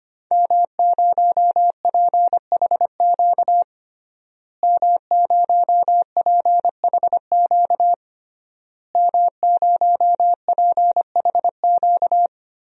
Morse code, heard as a keyed tone, about 25 wpm, 700 Hz. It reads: M0PHQ M0PHQ M0PHQ
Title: DXCC-100 Call Signs - 5 Characters - Repeated 3x - 25wpm